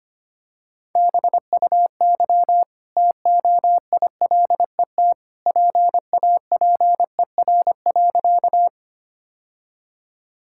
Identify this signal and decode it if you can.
Morse code, heard as a keyed tone, about 25 wpm, 700 Hz. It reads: BUY TOILET PAPER.
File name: Easy Sentences with Hints - 25wpm